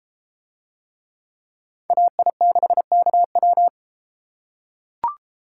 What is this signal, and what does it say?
Morse code, heard as a keyed tone, about 32 wpm, 700 Hz. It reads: AI6KW